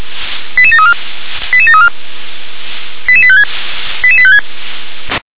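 Five-tone selective call (ZVEI), heard as quick runs of five tones, tones of about 70 ms. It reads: 88734 88734 88856 88856
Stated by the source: ZVEI five-tone selective call 88856 received by SDR.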